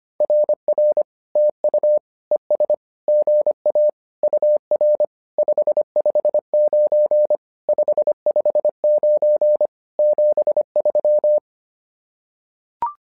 Morse code, heard as a keyed tone, about 25 wpm, 600 Hz. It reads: RR TU ES GA UR 559 559 73
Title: SOTA QSO Practice 25wpm